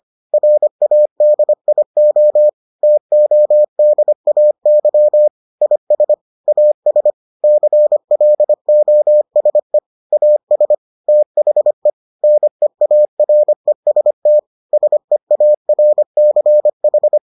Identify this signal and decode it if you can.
Morse code, heard as a keyed tone, about 25 wpm, 600 Hz. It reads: RADIO TODAY IS AS CLOSE AS THE NEAREST SEARCH